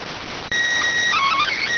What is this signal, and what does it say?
SSTV (slow-scan television), PD 290.